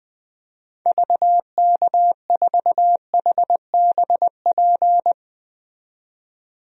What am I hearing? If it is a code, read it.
Morse code, heard as a keyed tone, about 20 wpm, 700 Hz. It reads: VK4HBP
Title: DXCC-100 Call Signs - 6 Characters - 20wpm